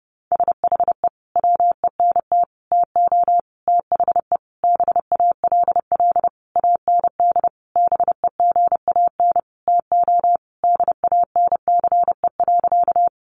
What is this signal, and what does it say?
Morse code, heard as a keyed tone, about 30 wpm, 700 Hz. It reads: SHE WENT TO THE BALL AND BEGAN TO DANCE.
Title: Sentences from Top 900 Words 30wpm